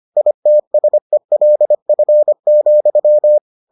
Morse code, heard as a keyed tone, about 25 wpm, 600 Hz. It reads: ITSELF,